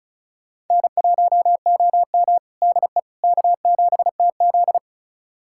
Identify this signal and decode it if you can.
Morse code, heard as a keyed tone, about 36 wpm, 700 Hz. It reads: N1OM DE K7TZ
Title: USA Callsign DE Callsign - 4 Characters - 35wpm